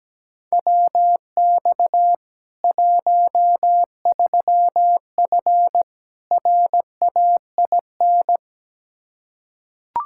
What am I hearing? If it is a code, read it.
Morse code, heard as a keyed tone, about 17 wpm, 700 Hz. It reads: WX 13F RAIN
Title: Practice QSO - Exchange 2 - Info - 17wpm